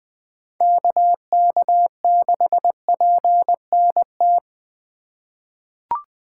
Morse code, heard as a keyed tone, about 20 wpm, 700 Hz. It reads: KK6PNT
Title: USA Call Signs - 6 Characters - 4x Character Spacing - 20wpm